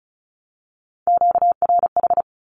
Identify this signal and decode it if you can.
Morse code, heard as a keyed tone, about 34 wpm, 700 Hz. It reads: QRH
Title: Q-Codes 35wpm